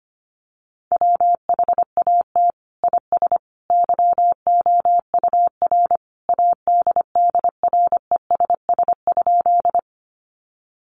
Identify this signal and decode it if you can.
Morse code, heard as a keyed tone, about 25 wpm, 700 Hz. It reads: WHAT IS YOUR ADDRESS?
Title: Sentences from Top 1000 Words 25wpm